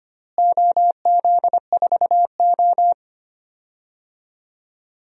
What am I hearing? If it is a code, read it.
Morse code, heard as a keyed tone, about 25 wpm, 700 Hz. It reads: OZ4O
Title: DXCC-100 Call Signs - 4 Characters - Repeated 2x - 25wpm